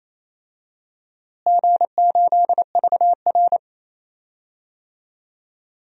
Morse code, heard as a keyed tone, about 28 wpm, 700 Hz. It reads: G8VR
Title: DXCC-100 Call Signs - 4 Characters - Repeated 3x - 28wpm